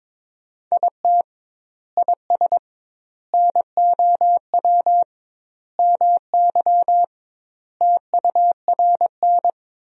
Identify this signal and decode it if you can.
Morse code, heard as a keyed tone, about 22 wpm, 700 Hz. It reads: IT IS NOW MY TURN
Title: 2x Word-Spacing - Sentences from Top 200 Words - 22wpm